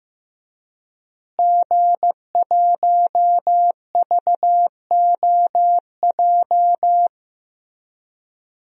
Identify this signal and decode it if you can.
Morse code, heard as a keyed tone, about 15 wpm, 700 Hz. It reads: G1VOJ
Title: DXCC-100 Call Signs - 5 Characters - Repeated 3x - 15wpm